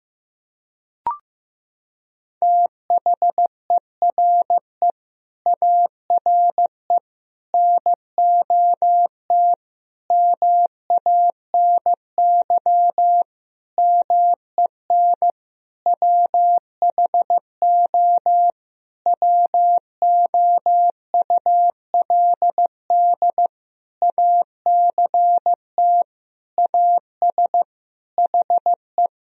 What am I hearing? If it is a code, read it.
Morse code, heard as a keyed tone, about 16 wpm, 700 Hz. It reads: THERE ARE NOT MANY MEN WHO WOULD ACT AS HE
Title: Sentences from Top 1000 Words - Review Part 1 - 15wpm